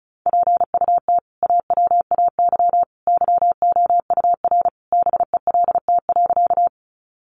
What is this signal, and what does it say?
Morse code, heard as a keyed tone, about 34 wpm, 700 Hz. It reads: PUT AWAY YOUR BELT.
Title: Sentences from Top 1400 Words 35wpm